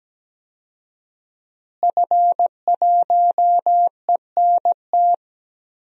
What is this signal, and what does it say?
Morse code, heard as a keyed tone, about 17 wpm, 700 Hz. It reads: F1ENT
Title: DXCC-100 Call Signs - 5 Characters - Repeated 2x - 17wpm